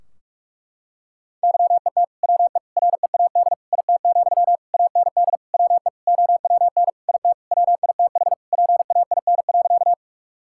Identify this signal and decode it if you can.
Morse code, heard as a keyed tone, about 46 wpm, 700 Hz. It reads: YET WE READ IT, AND WE OWN IT WITH PAIN.
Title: Sentences from Top 1600 Words 45wpm